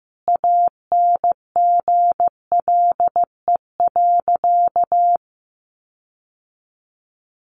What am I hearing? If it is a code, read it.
Morse code, heard as a keyed tone, about 15 wpm, 700 Hz. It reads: ANGLE.
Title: Sentences from Top 1500 Words 15wpm